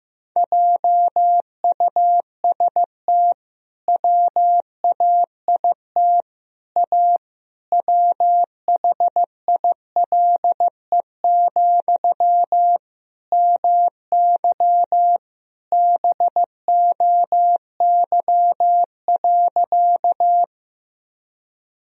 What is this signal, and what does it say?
Morse code, heard as a keyed tone, about 15 wpm, 700 Hz. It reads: JUST WAIT A WHILE, MY BOY.
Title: Sentences from Top 1000 Words - Review Part 1 - 15wpm